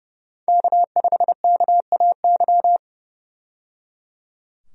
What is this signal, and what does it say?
Morse code, heard as a keyed tone, about 29 wpm, 700 Hz. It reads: K5KAY
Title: USA Call Signs - 5 Characters - 30wpm